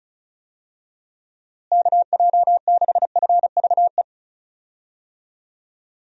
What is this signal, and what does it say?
Morse code, heard as a keyed tone, about 34 wpm, 700 Hz. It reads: KJ6FVE
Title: USA Call Signs - 6 Characters - Repeated 3x - 35wpm